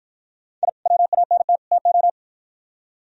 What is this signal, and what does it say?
Morse code, heard as a keyed tone, about 51 wpm, 700 Hz. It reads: I WANT TO